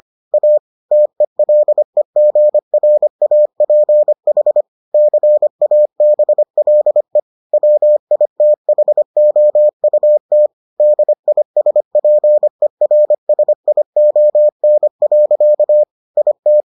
Morse code, heard as a keyed tone, about 25 wpm, 600 Hz. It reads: A TELEGRAPH CABLE WITHOUT DISPERSION. IT